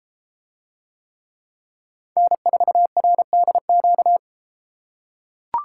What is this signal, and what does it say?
Morse code, heard as a keyed tone, about 32 wpm, 700 Hz. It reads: N4RDQ